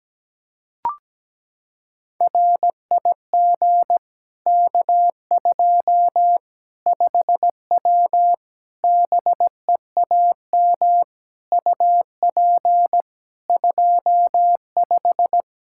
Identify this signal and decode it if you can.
Morse code, heard as a keyed tone, about 17 wpm, 700 Hz. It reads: RIG K2 5W BEAM UP 25
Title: Practice QSO - Exchange 2 - Info - 17wpm